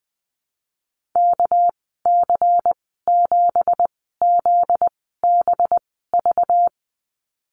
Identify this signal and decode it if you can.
Morse code, heard as a keyed tone, about 20 wpm, spaced as for 14 wpm, 700 Hz. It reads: KC7ZBV
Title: 245 Lesson 39g - Letter z - Call Signs 20wpm